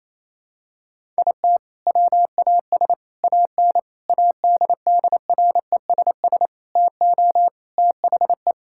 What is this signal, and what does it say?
Morse code, heard as a keyed tone, about 27 wpm, 700 Hz. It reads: IT WAS AN ADDRESS TO THE
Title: Sentences from Top 1000 Words - Review Part 6 - 28wpm